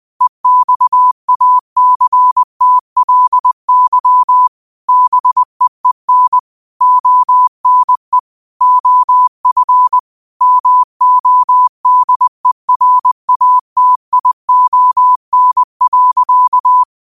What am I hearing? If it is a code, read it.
Morse code, heard as a keyed tone, about 20 wpm, 1000 Hz. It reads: EXACTLY BEEN ONE OF MODERATION.